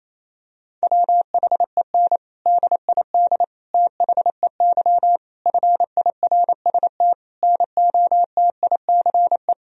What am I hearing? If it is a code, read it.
Morse code, heard as a keyed tone, about 28 wpm, 700 Hz. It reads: WHEN DID THEY FIRST NOTICE
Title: Sentences from Top 1000 Words - Review Part 2 - 28wpm